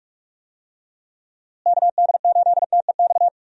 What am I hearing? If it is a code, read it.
Morse code, heard as a keyed tone, about 43 wpm, 700 Hz. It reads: KD8TEX
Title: USA Call Signs - 6 Characters - Repeated 3x - 45wpm